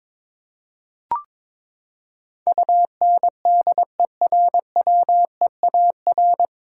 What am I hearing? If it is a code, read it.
Morse code, heard as a keyed tone, about 22 wpm, 700 Hz. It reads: UNDERWEAR
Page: Morse Code Ninja practice 22wpm